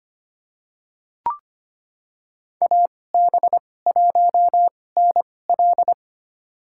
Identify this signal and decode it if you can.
Morse code, heard as a keyed tone, about 25 wpm, spaced as for 18 wpm, 700 Hz. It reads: AB1NL
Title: USA Call Signs - 5 Characters - 2x Character Spacing - 25wpm